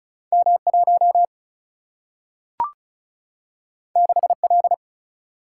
Morse code, heard as a keyed tone, about 36 wpm, 700 Hz. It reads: M1 6L
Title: Single-Letter Single-Number 35wpm